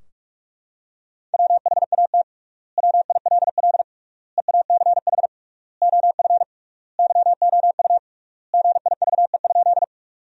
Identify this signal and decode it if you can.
Morse code, heard as a keyed tone, about 41 wpm, 700 Hz. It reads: WHAT WILL EACH OF YOU GIVE?
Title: 3x Word-Spacing - Sentences from Top 200 Words - 45wpm